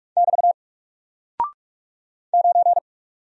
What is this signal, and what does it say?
Morse code, heard as a keyed tone, about 44 wpm, 700 Hz. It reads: <BT> 9
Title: Single Character (Letter:Number:Punctuation:Prosign) 45wpm